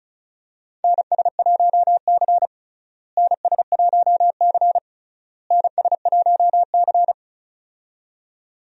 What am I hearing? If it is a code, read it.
Morse code, heard as a keyed tone, about 36 wpm, 700 Hz. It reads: NS1C NS1C NS1C
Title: USA Call Signs - 4 Characters - Repeated 3x - 35wpm